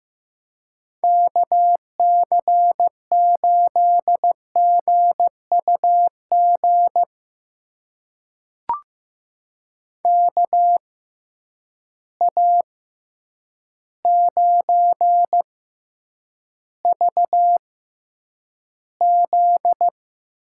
Morse code, heard as a keyed tone, about 15 wpm, spaced as for 8 wpm, 700 Hz. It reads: KC8GUG KA9VZ
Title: USA Call Signs - 6 Characters - 6x Character Spacing - 15wpm